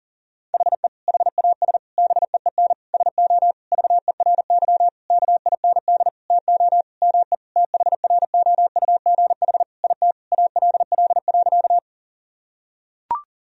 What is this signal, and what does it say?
Morse code, heard as a keyed tone, about 37 wpm, 700 Hz. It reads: HE HAS BEEN SO VERY KIND TO ME THROUGH IT ALL.
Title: Sentences from Top 1000 Words - Review Part 1 - 40wpm